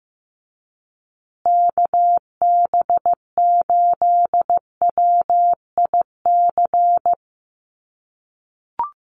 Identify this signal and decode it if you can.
Morse code, heard as a keyed tone, about 15 wpm, 700 Hz. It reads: KB8WIC
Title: USA Call Signs Repeated 2x 15wpm